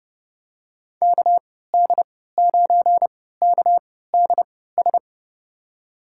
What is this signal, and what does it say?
Morse code, heard as a keyed tone, about 30 wpm, spaced as for 17 wpm, 700 Hz. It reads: KD9KDS
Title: USA Call Signs - 6 Characters - 3x Character Spacing - 30wpm